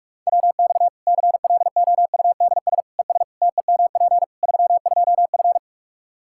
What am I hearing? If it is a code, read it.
Morse code, heard as a keyed tone, about 46 wpm, 700 Hz. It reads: WX CLOUDS ES TEMP 32F
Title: Practice QSO - Exchange 2 - Info - 45wpm